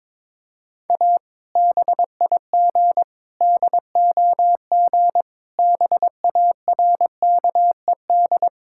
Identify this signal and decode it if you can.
Morse code, heard as a keyed tone, about 22 wpm, 700 Hz. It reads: A BIG DOG BARKED
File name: practice / 22wpm / Sentences with 2-Word Phrases - 22wpm